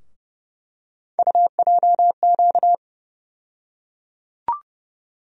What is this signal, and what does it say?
Morse code, heard as a keyed tone, about 30 wpm, 700 Hz. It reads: UJQ
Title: USA Call Sign Postfixes - 3 Letters 30wpm